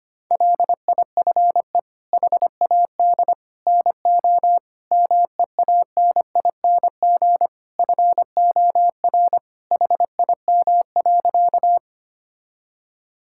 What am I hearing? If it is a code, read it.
Morse code, heard as a keyed tone, about 25 wpm, 700 Hz. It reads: LIFE HAD NO MEANING FOR HIM.
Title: Sentences from Top 1000 Words - Review Part 6 - 25wpm